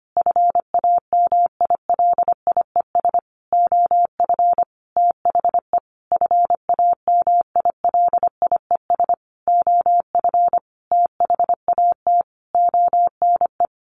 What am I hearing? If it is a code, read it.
Morse code, heard as a keyed tone, about 25 wpm, 700 Hz. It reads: FAMILIES OF THE FAMILIES OF THAT ONE
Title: A Sound Of Thunder 25wpm